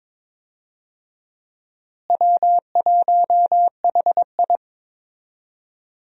Morse code, heard as a keyed tone, about 21 wpm, 700 Hz. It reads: W1HI
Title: USA Call Signs - 4 Characters - Repeated 3x - 22wpm